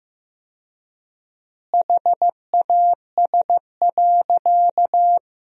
Morse code, heard as a keyed tone, about 15 wpm, 700 Hz. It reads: HAS.
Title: 4x Word-Spacing - Sentences from Top 500 Words - 15wpm